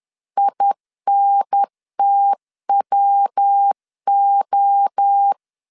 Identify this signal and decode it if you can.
Morse code, heard as a keyed tone, about 10 wpm, 800 Hz. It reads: INTWO
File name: CW / DET / 1 3 1500C S C 10wpm 3335